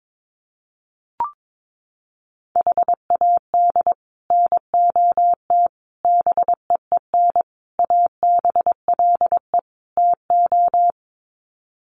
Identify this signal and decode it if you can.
Morse code, heard as a keyed tone, about 22 wpm, 700 Hz. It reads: HAD NOT BEEN ABLE TO
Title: Sets of 5 Words Encore 22wpm